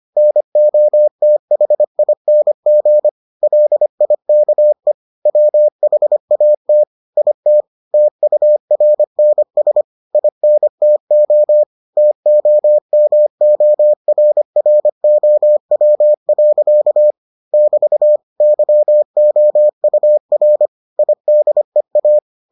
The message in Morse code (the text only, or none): NOTHING LIKE WHAT IT TURNS INTO TOMORROW. <BT> YOUR IDEA